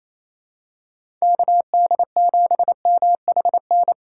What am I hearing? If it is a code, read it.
Morse code, heard as a keyed tone, about 28 wpm, 700 Hz. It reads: KD7MHN
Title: USA Call Signs - 3x Character Spacing - 28wpm